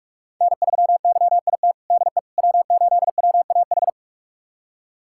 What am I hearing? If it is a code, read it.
Morse code, heard as a keyed tone, about 46 wpm, 700 Hz. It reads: N3YIT DE W8WAH
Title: USA Callsign DE Callsign - 5 Characters - 45wpm